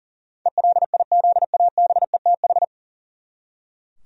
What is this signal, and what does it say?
Morse code, heard as a keyed tone, about 40 wpm, 700 Hz. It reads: ELIZABETH
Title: USA Common Names - 40wpm